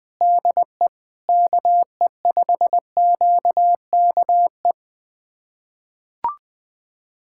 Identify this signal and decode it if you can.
Morse code, heard as a keyed tone, about 21 wpm, 700 Hz. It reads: DE KE5QKE
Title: USA Callsign DE Callsign - 6 Characters - 20wpm